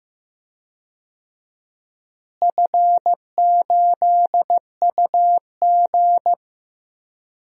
Morse code, heard as a keyed tone, about 15 wpm, 700 Hz. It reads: F8UG
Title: DXCC-100 Call Signs - 4 Characters - Repeated 2x - 15wpm